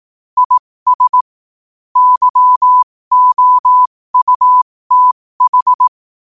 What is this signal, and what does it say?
Morse code, heard as a keyed tone, about 18 wpm, spaced as for 15 wpm, 1000 Hz. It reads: IS YOUTH
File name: qotdfiles / 15wpm